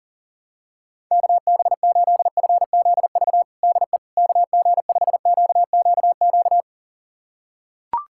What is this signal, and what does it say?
Morse code, heard as a keyed tone, about 40 wpm, 700 Hz. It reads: KB8FZV DE KG5QQQ
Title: USA Callsign DE Callsign - 6 Characters - 40wpm